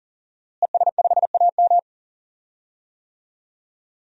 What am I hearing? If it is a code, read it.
Morse code, heard as a keyed tone, about 40 wpm, 700 Hz. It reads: ES5AM
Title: DXCC-100 Call Signs - 5 Characters - Repeated 3x - 40wpm